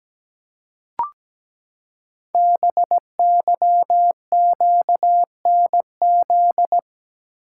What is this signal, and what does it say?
Morse code, heard as a keyed tone, about 17 wpm, 700 Hz. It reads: BYQNZ
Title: Code Groups - 5 Letters - 17wpm